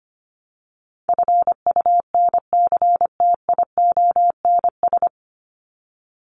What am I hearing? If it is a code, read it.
Morse code, heard as a keyed tone, about 25 wpm, 700 Hz. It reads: FUNCTIONS
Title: ES:S Suffix - Top 100 Words with Base Words 25wpm